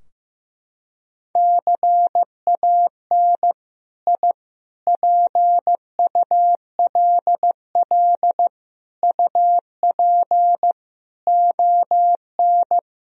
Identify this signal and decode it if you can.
Morse code, heard as a keyed tone, about 15 wpm, 700 Hz. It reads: CAN I PULL UP ON
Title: Sentences from Top 1000 Words - Review Part 2 - 15wpm